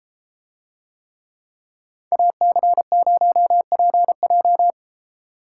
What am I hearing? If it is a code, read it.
Morse code, heard as a keyed tone, about 33 wpm, 700 Hz. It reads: AC0PJ